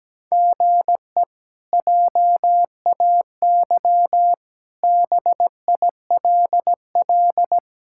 Morse code, heard as a keyed tone, about 17 wpm, 700 Hz. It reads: GE JAY BILL
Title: K1USN SST QSO Practice - 17wpm